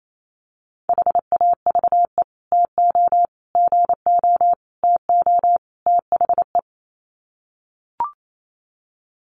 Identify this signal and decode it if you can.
Morse code, heard as a keyed tone, about 28 wpm, 700 Hz. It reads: HAVE TO GO TO THE
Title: Sets of 5 Words Encore 28wpm